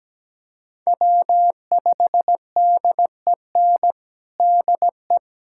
Morse code, heard as a keyed tone, about 17 wpm, 700 Hz. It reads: W5DEN DE
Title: USA Callsign DE Callsign 17wpm